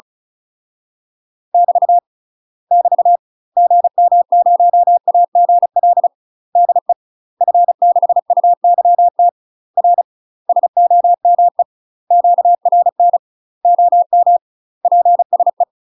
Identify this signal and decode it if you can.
Morse code, heard as a keyed tone, about 34 wpm, 700 Hz. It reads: <BT> <BT> GM0AGL DE F6UYT R SOME QRN OM PSE
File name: UK MASTER QSOs TRAINING / 35WPM / UK MASTER QSOs 0030